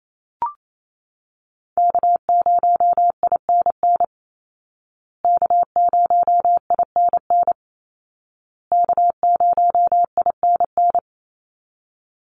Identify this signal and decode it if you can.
Morse code, heard as a keyed tone, about 28 wpm, 700 Hz. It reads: K0INN K0INN K0INN
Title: Call Signs Repeated 2x - 28wpm